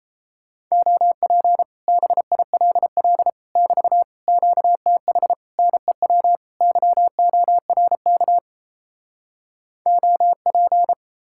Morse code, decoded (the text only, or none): OP BILL <BT> QTH NEW YORK OP